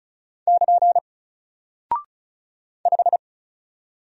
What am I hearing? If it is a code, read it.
Morse code, heard as a keyed tone, about 34 wpm, 700 Hz. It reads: <KN> 5
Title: Single Character (Letter:Number:Punctuation:Prosign) 35wpm